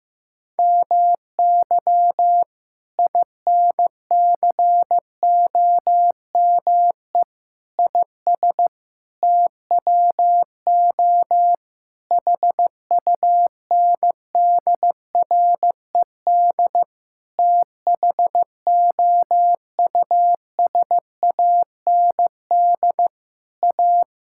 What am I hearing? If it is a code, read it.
Morse code, heard as a keyed tone, about 15 wpm, 700 Hz. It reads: MY INCOME IS TWO HUNDRED THOUSAND A
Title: Sentences from Top 1000 Words - Review Part 5 - 15wpm